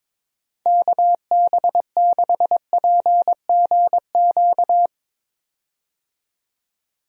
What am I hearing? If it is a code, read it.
Morse code, heard as a keyed tone, about 22 wpm, 700 Hz. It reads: KB6PGQ
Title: USA Call Signs - 6 Characters - 22wpm